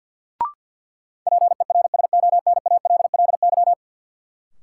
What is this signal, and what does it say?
Morse code, heard as a keyed tone, about 52 wpm, 700 Hz. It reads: PERSONALLY
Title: Sentences from Top 1400 Words - New Word Drill - 50wpm